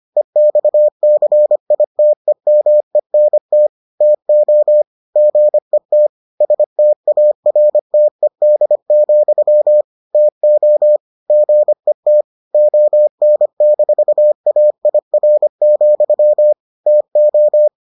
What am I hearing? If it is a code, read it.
Morse code, heard as a keyed tone, about 25 wpm, 600 Hz. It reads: EXCITEMENT TO GET STARTED, TO GET ON-AIR, TO